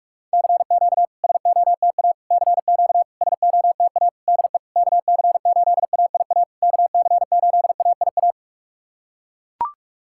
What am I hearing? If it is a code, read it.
Morse code, heard as a keyed tone, about 42 wpm, 700 Hz. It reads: CQ SOTA CQ SOTA DE KC8AIA KC8AIA
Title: SOTA QSO Practice 45wpm